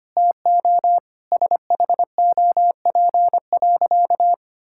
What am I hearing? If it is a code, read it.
Morse code, heard as a keyed tone, about 25 wpm, 700 Hz. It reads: TO SHOP.
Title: Sentences from Top 1600 Words 25wpm